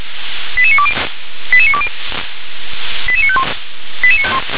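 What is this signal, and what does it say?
Five-tone selective call (ZVEI), heard as quick runs of five tones, tones of about 70 ms. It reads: 88920 88920 88951
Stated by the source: ZVEI five-tone call received on SDR.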